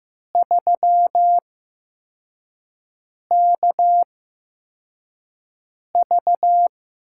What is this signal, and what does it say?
Morse code, heard as a keyed tone, about 15 wpm, spaced as for 4 wpm, 700 Hz. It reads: 3KV
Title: USA Call Signs - 8x Character Spacing - 15wpm